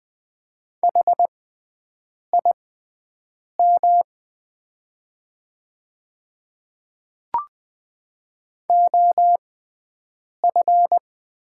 Morse code, heard as a keyed tone, about 20 wpm, spaced as for 7 wpm, 700 Hz. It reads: HIM OF
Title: 6x Character Spacing - Top 100 Words - 20wpm